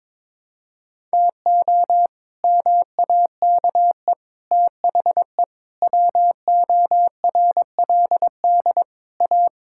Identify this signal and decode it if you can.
Morse code, heard as a keyed tone, about 22 wpm, 700 Hz. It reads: TO MAKE THE WORLD A
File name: practice / 22wpm / Sets of 7 Words - 22wpm